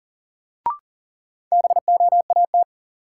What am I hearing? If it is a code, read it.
Morse code, heard as a keyed tone, about 40 wpm, 700 Hz. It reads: BOAT
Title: Sentences from Top 500 Words - New Word Drill - 40wpm